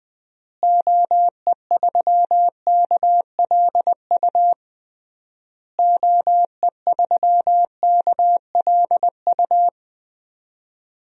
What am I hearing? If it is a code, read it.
Morse code, heard as a keyed tone, about 20 wpm, 700 Hz. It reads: OE3KLU OE3KLU
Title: DXCC-100 Call Signs - 6 Characters - Repeated 3x - 20wpm